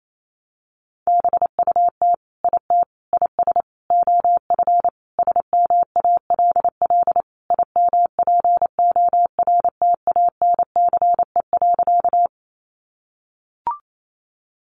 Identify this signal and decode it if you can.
Morse code, heard as a keyed tone, about 28 wpm, 700 Hz. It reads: BUT IT IS OF SMALL IMPORTANCE.
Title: Sentences from Top 1200 Words 28wpm